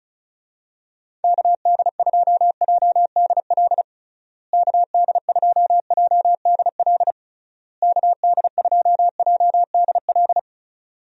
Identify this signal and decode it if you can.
Morse code, heard as a keyed tone, about 36 wpm, 700 Hz. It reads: KD2JDL KD2JDL KD2JDL
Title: USA Call Signs - 6 Characters - Repeated 3x - 35wpm